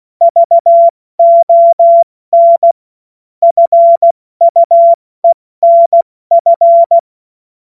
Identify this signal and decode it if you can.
Morse code, heard as a keyed tone, about 16 wpm, 680 Hz. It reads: VON FUENF
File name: Morsekurs / Witze / Witz 04. Mai 2021, 70 F80